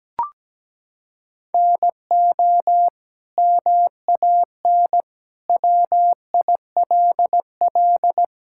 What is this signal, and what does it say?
Morse code, heard as a keyed tone, about 17 wpm, 700 Hz. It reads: NO MAN WILL
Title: Sentences from Top 1000 Words - Review Part 6 - 17wpm